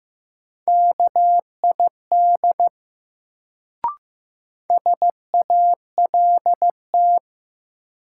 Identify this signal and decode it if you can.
Morse code, heard as a keyed tone, about 16 wpm, 700 Hz. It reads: KID SALT
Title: Sets of 5 Words - New Word Drill - 15wpm